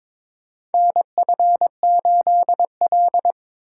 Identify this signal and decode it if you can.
Morse code, heard as a keyed tone, about 22 wpm, 700 Hz. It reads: NF8L
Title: USA Call Signs - 4 Characters - 6x Character Spacing - 22wpm